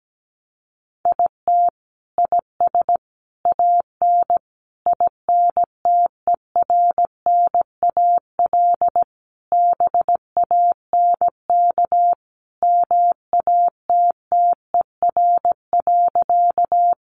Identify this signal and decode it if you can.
Morse code, heard as a keyed tone, about 17 wpm, 700 Hz. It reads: IT IS AN INTERNAL BANK MATTER.
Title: Sentences from Top 1400 Words 17wpm